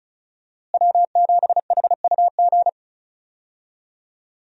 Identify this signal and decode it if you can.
Morse code, heard as a keyed tone, about 34 wpm, 700 Hz. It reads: W7HUG
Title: USA Call Signs - 5 Characters - Repeated 2x - 35wpm